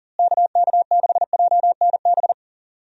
Morse code, heard as a keyed tone, about 40 wpm, 700 Hz. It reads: KK6JNB
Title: USA Call Signs - 6 Characters - Repeated 3x - 40wpm